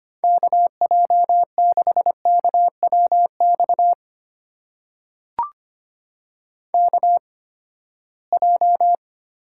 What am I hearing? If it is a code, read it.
Morse code, heard as a keyed tone, about 25 wpm, spaced as for 17 wpm, 700 Hz. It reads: KJ6KWX KJ
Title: USA Call Signs - 6 Characters - 8x Character Spacing - 25wpm